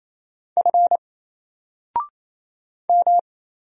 Morse code, heard as a keyed tone, about 27 wpm, 700 Hz. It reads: F M
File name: Single-Letter-Number-28wpm